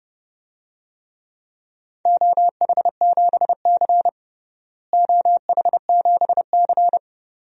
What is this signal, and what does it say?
Morse code, heard as a keyed tone, about 30 wpm, 700 Hz. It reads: OH7C OH7C
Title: DXCC-100 Call Signs - 4 Characters - Repeated 2x - 30wpm